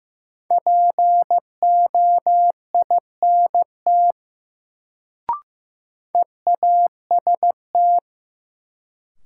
Morse code, heard as a keyed tone, about 16 wpm, 700 Hz. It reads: POINT EAST
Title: Sets of 3 Words - New Word Drill - 15wpm